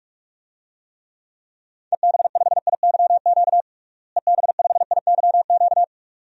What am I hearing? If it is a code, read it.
Morse code, heard as a keyed tone, about 46 wpm, 700 Hz. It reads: EB5IYQ EB5IYQ
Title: DXCC-100 Call Signs - 6 Characters - Repeated 2x - 45wpm